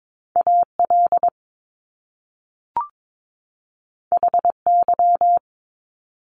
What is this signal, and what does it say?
Morse code, heard as a keyed tone, about 22 wpm, 700 Hz. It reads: AL HY
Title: Top Two Letter Combinations in Words 22wpm